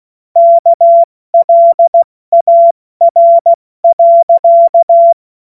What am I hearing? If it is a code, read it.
Morse code, heard as a keyed tone, about 16 wpm, spaced as for 14 wpm, 680 Hz. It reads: KLAR.
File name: Morsekurs / Witze / Witz 19. Oktober 2021, 70 F80